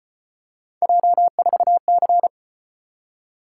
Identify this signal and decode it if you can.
Morse code, heard as a keyed tone, about 32 wpm, 700 Hz. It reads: J4C